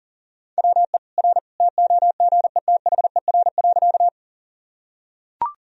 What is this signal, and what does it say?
Morse code, heard as a keyed tone, about 40 wpm, 700 Hz. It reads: WE R TOGETHER.
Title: Sentences from Top 300 Words and Abbreviations 40wpm